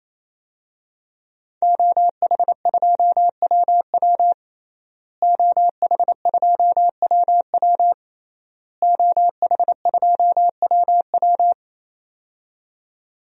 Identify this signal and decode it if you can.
Morse code, heard as a keyed tone, about 28 wpm, 700 Hz. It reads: OH2WW OH2WW OH2WW
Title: DXCC-100 Call Signs - 5 Characters - Repeated 3x - 28wpm